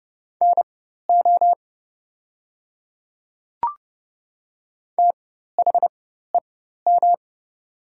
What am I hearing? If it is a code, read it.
Morse code, heard as a keyed tone, about 30 wpm, spaced as for 14 wpm, 700 Hz. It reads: NO THEM
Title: 4x Character Spacing - Top 100 Words - 30wpm